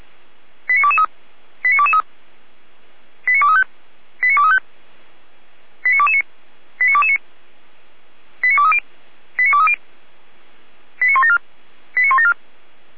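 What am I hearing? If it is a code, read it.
Five-tone selective call (ZVEI), heard as quick runs of five tones, tones of about 70 ms. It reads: 89223 89223 89235 89235 89229 89229 89230 89230 89174 89174
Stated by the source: ZVEI five-tone selective call received with an SDR.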